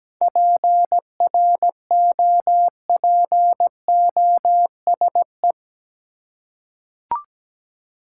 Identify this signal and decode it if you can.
Morse code, heard as a keyed tone, about 17 wpm, 700 Hz. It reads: PROPOSE
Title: Seven Letter Words - 17wpm